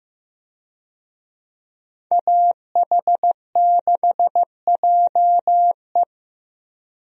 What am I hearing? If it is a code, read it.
Morse code, heard as a keyed tone, about 15 wpm, 700 Hz. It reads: AH6JE
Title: USA Call Signs - 5 Characters - 15wpm